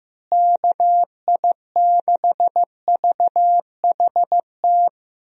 Morse code, heard as a keyed tone, about 15 wpm, 700 Hz. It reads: KI6VHT
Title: USA Call Signs - 6 Characters - Repeated 3x - 15wpm